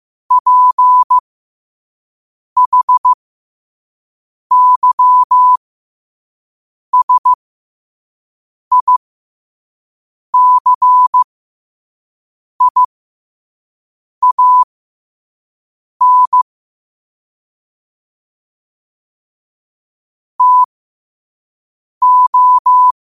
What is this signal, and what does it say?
Morse code, heard as a keyed tone, about 15 wpm, spaced as for 5 wpm, 1000 Hz. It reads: PHYSICIAN TO
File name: qotdfiles / 5wpm